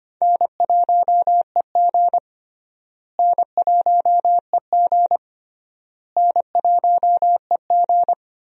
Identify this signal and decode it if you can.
Morse code, heard as a keyed tone, about 25 wpm, 700 Hz. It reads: N1EG N1EG N1EG
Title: USA Call Signs - 4 Characters - Repeated 3x - 25wpm